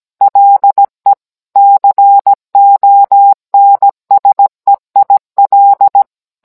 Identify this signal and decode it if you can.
Morse code, heard as a keyed tone, about 17 wpm, 800 Hz. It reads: LE CONSEIL